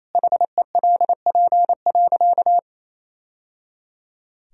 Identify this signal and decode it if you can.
Morse code, heard as a keyed tone, about 28 wpm, 700 Hz. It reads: HELP.
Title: Sentences from Top 1000 Words - Review Part 2 - 28wpm